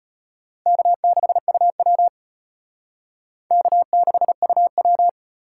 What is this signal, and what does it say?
Morse code, heard as a keyed tone, about 34 wpm, 700 Hz. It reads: K6UW K6UW